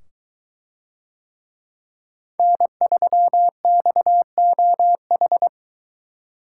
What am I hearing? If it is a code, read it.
Morse code, heard as a keyed tone, about 23 wpm, 700 Hz. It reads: N3XOH